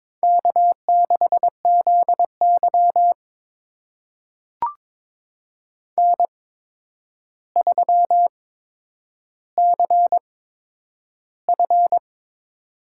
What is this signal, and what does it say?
Morse code, heard as a keyed tone, about 22 wpm, spaced as for 9 wpm, 700 Hz. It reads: K6ZY N3CF
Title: USA Call Signs - 4 Characters - 8x Character Spacing - 22wpm